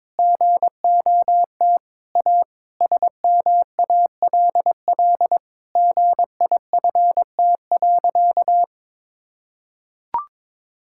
Morse code, heard as a keyed tone, about 21 wpm, 700 Hz. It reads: GOT A SMALL GIFT.
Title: Sentences from Top 1000 Words - Review Part 6 - 22wpm